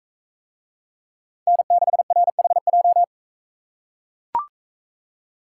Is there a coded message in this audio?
Yes: Morse code from 1.5 to 3.0 s, heard as a keyed tone, about 43 wpm, 700 Hz. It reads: N6RHJ